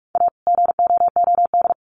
Morse code, heard as a keyed tone, about 44 wpm, 700 Hz. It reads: A GOOD
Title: Sets of 2 Words 45wpm